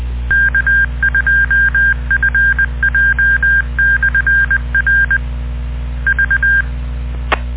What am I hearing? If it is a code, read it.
Morse code, heard as a keyed tone, about 20 wpm, 1600 Hz. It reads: K2FJ/R V